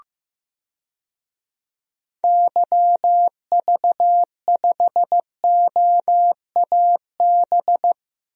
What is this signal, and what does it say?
Morse code, heard as a keyed tone, about 15 wpm, 700 Hz. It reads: YV5OAB
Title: DXCC-100 Call Signs - 6 Characters - Repeated 3x - 15wpm